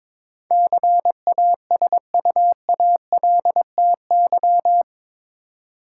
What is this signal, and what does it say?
Morse code, heard as a keyed tone, about 22 wpm, 700 Hz. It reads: CASUALTY
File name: practice / 22wpm / Eight Letter Words - 22wpm